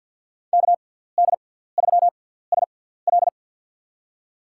Morse code, heard as a keyed tone, about 53 wpm, spaced as for 17 wpm, 700 Hz. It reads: KD3SL
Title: USA Call Signs - 5 Characters - 6x Character Spacing - 50wpm